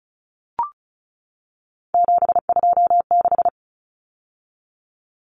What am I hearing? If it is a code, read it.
Morse code, heard as a keyed tone, about 34 wpm, 700 Hz. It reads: 726
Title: Numbers - 3-Digit 35wpm